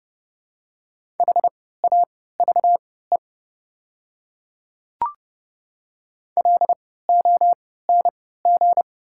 Morse code, heard as a keyed tone, about 29 wpm, spaced as for 17 wpm, 690 Hz. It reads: HAVE LONG
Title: 3x Character Spacing - Top 100 Words - 30wpm